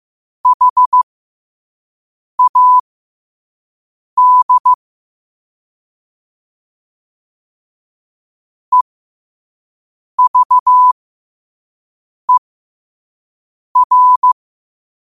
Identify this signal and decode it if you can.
Morse code, heard as a keyed tone, about 15 wpm, spaced as for 5 wpm, 1000 Hz. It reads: HAD EVER
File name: qotdfiles / 5wpm